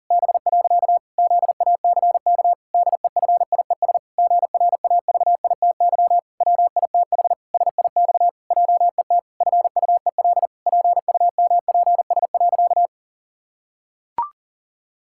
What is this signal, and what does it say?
Morse code, heard as a keyed tone, about 40 wpm, 700 Hz. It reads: B. ZACK DEFIES GRAVITY WITH SIX JET FUEL PUMPS.
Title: Pangram Challenge v2 - 40wpm